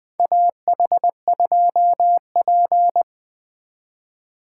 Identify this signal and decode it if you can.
Morse code, heard as a keyed tone, about 20 wpm, 700 Hz. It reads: AH2P
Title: USA Call Signs - 4 Characters - 3x Character Spacing - 20wpm